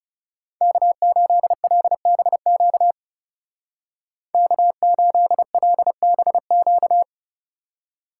Morse code, heard as a keyed tone, about 33 wpm, 700 Hz. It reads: K8LBQ K8LBQ
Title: Speed Racing - USA 1x3 Call Signs 30wpm